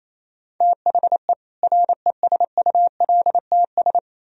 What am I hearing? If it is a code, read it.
Morse code, heard as a keyed tone, about 38 wpm, spaced as for 33 wpm, 700 Hz. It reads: THE RESULTS
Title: Speed-Racing - Sets of 5 Words 28wpm